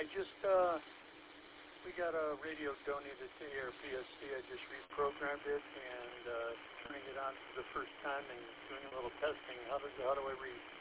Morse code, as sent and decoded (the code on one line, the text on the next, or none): none